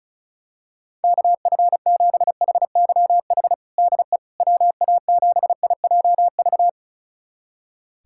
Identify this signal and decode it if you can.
Morse code, heard as a keyed tone, about 34 wpm, 690 Hz. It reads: KF7HYH DE WA7IJV
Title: USA Callsign DE Callsign - 6 Characters - 35wpm